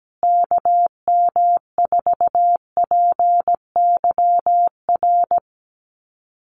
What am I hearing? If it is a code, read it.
Morse code, heard as a keyed tone, about 17 wpm, 700 Hz. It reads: KM4PYR
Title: USA Call Signs Repeated 2x 17wpm